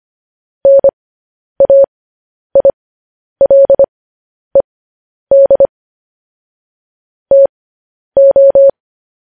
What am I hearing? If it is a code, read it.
Morse code, heard as a keyed tone, about 25 wpm, spaced as for 10 wpm, 550 Hz. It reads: NAILED TO
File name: podcast / News headlines-1x-25x10